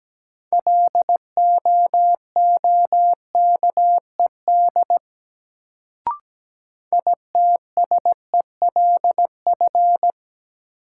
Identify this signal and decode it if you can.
Morse code, heard as a keyed tone, about 17 wpm, 700 Hz. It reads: LOOKED ITSELF
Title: Sentences from Top 800 Words - New Word Drill - 17wpm